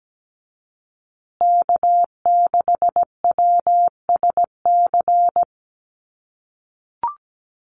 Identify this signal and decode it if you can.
Morse code, heard as a keyed tone, about 17 wpm, 700 Hz. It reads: K6WSC
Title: USA Call Signs Repeated 2x 17wpm